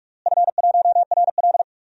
Morse code, heard as a keyed tone, about 45 wpm, 700 Hz. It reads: F1RL
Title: DXCC-100 Call Signs - 4 Characters - Repeated 2x - 45wpm